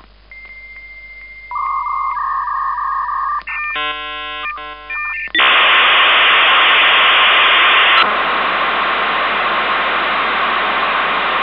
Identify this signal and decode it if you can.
Dial-up modem handshake.